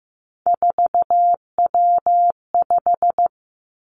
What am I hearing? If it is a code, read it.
Morse code, heard as a keyed tone, about 15 wpm, 700 Hz. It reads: 4W5
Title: Three Challenging Letter-Number Combinations 15wpm